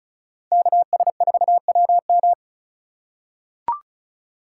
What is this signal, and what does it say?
Morse code, heard as a keyed tone, about 36 wpm, 700 Hz. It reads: KS4WM
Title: USA Call Signs - 5 Characters - 2x Character Spacing - 35wpm